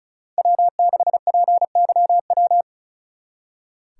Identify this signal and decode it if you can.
Morse code, heard as a keyed tone, about 44 wpm, 700 Hz. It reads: W6PYW
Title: Speed Racing - USA 1x3 Call Signs 35wpm